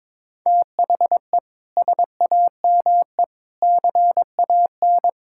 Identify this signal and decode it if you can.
Morse code, heard as a keyed tone, about 22 wpm, 700 Hz. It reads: THE SAME CAN
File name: practice / 22wpm / Sets of 6 Words - 22wpm